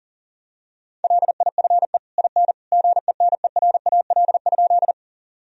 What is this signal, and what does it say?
Morse code, heard as a keyed tone, about 40 wpm, 700 Hz. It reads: LIFE IN GENERAL?
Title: Sentences from Top 1000 Words - Review Part 3 - 40wpm